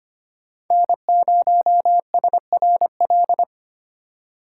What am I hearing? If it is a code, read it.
Morse code, heard as a keyed tone, about 25 wpm, 700 Hz. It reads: N0SRL
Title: USA Call Signs - 5 Characters - 2x Character Spacing - 25wpm